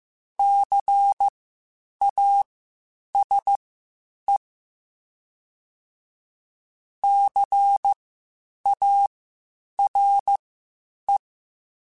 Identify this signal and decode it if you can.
Morse code, heard as a keyed tone, about 15 wpm, spaced as for 8 wpm, 800 Hz. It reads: CASE CARE
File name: Spaced Alphas / Spaced Alphas mp3s / Cs sp @13